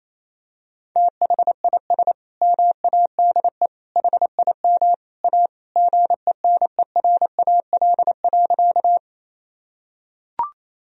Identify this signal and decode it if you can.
Morse code, heard as a keyed tone, about 28 wpm, 700 Hz. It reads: THIS MADE HIM A GENERAL.
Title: Sentences from Top 1000 Words - Review Part 3 - 28wpm